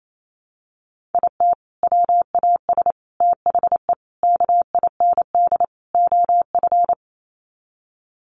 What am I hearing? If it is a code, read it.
Morse code, heard as a keyed tone, about 28 wpm, 700 Hz. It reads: IT WAS THE KIND OF
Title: Sets of 5 Words Encore2 28wpm